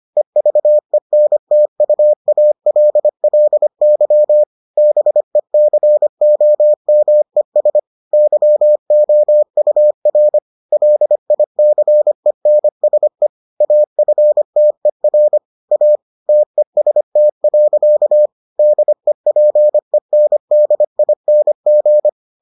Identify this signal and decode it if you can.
Morse code, heard as a keyed tone, about 25 wpm, 600 Hz. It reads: EVENTUALLY BECOMES YOUR LICENSE AFTER A TEST. DEPENDING